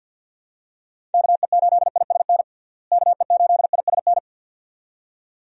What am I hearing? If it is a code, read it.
Morse code, heard as a keyed tone, about 50 wpm, 700 Hz. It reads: KE8ISN KE8ISN
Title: USA Call Signs - 6 Characters - Repeated 2x - 50wpm